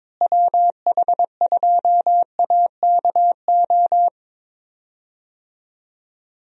Morse code, heard as a keyed tone, about 22 wpm, 700 Hz. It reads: WH2AKO
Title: USA Call Signs - 6 Characters - Repeated 2x - 22wpm